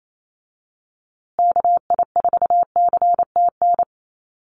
Morse code, heard as a keyed tone, about 28 wpm, 700 Hz. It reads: KI4CTN
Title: USA Call Signs Repeated 2x 28wpm